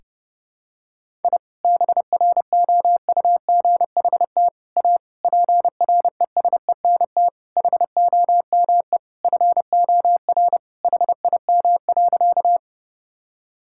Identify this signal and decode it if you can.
Morse code, heard as a keyed tone, about 30 wpm, 700 Hz. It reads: I BROUGHT A PRESENT HOME FOR HIM.
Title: Sentences from Top 1000 Words - Review Part 5 - 30wpm